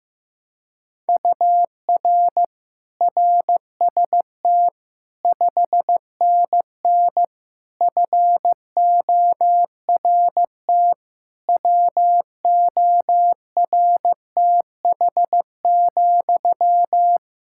Morse code, heard as a keyed tone, about 16 wpm, 700 Hz. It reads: UR RST 5NN FORT WORTH,
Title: Practice QSO - Exchange 1 - Beginning - 15wpm